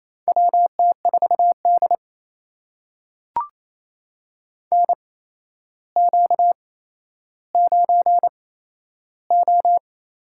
Morse code, heard as a keyed tone, about 28 wpm, spaced as for 12 wpm, 700 Hz. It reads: WT4D NQ9O
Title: USA Call Signs - 4 Characters - 8x Character Spacing - 28wpm